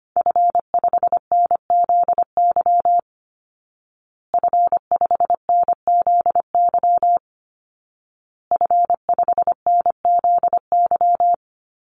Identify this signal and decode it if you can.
Morse code, heard as a keyed tone, about 25 wpm, 700 Hz. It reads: F5NZY F5NZY F5NZY
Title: Call Signs Repeated 2x - 25wpm